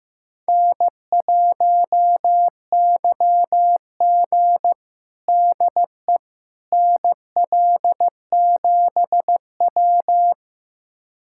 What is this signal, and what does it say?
Morse code, heard as a keyed tone, about 16 wpm, 700 Hz. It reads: N1YG DE NL7W
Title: USA Callsign DE Callsign - 4 Characters - 15wpm